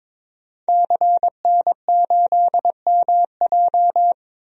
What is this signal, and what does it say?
Morse code, heard as a keyed tone, about 22 wpm, 700 Hz. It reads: CN8MJ
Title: DXCC-100 Call Signs - 5 Characters - Repeated 2x - 22wpm